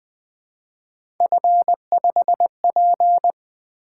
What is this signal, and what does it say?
Morse code, heard as a keyed tone, about 20 wpm, 700 Hz. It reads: F5P
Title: Three Letter-Number Combinations - 20wpm